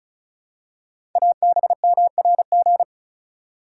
Morse code, heard as a keyed tone, about 36 wpm, 690 Hz. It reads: ABMRG
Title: Code Groups - 5 Letters - 35wpm